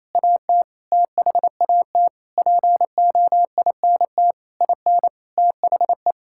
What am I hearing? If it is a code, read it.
Morse code, heard as a keyed tone, about 27 wpm, 700 Hz. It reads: AT THAT POINT IN THE
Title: Sentences from Top 1000 Words - Review Part 5 - 28wpm